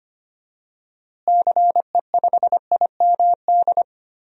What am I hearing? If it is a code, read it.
Morse code, heard as a keyed tone, about 25 wpm, 700 Hz. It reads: CE5IMD
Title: DXCC-100 Call Signs - 6 Characters - Repeated 2x - 25wpm